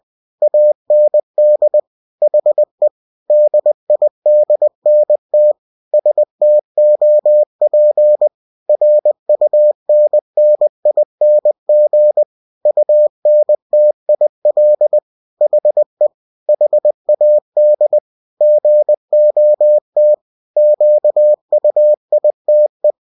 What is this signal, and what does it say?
Morse code, heard as a keyed tone, about 20 wpm, 600 Hz. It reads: AND HE DIDNT STOP RUNNING UNTIL HE HAD GOT QUITE